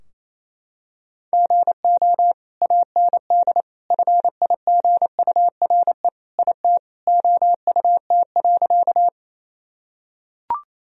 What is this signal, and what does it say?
Morse code, heard as a keyed tone, about 28 wpm, 700 Hz. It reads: GO AND FIGURE IT OUT.
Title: Sentences from Top 1000 Words - Review Part 2 - 28wpm